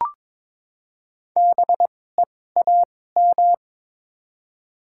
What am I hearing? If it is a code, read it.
Morse code, heard as a keyed tone, about 23 wpm, spaced as for 16 wpm, 700 Hz. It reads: BEAM
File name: Top 90 QSO Elements - 2x Character Spacing - 22wpm